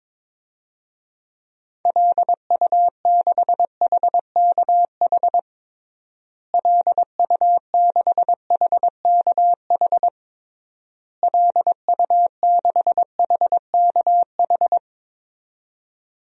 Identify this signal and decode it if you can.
Morse code, heard as a keyed tone, about 21 wpm, 700 Hz. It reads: LU6HKH LU6HKH LU6HKH
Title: DXCC-100 Call Signs - 6 Characters - Repeated 3x - 22wpm